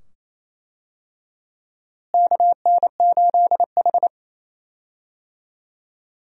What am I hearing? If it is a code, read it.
Morse code, heard as a keyed tone, about 28 wpm, 700 Hz. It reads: KN8H
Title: USA Call Signs - 4 Characters - Repeated 2x - 28wpm